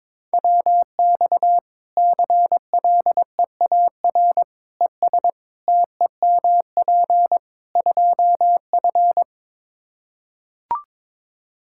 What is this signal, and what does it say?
Morse code, heard as a keyed tone, about 22 wpm, 700 Hz. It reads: WX CLEAR ES TEMP 2F
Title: Practice QSO - Exchange 2 - Info - 22wpm